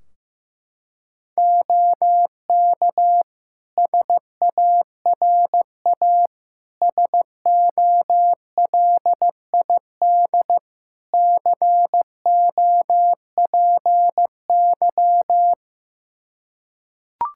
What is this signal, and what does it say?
Morse code, heard as a keyed tone, about 15 wpm, 700 Hz. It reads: OK SARA SOLID COPY
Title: Practice QSO - Exchange 3 - More Info - 15wpm